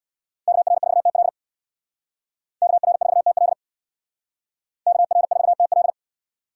Morse code, heard as a keyed tone, about 62 wpm, 700 Hz. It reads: KR4TL KR4TL KR4TL
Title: Speed Racing - USA Call Signs 50wpm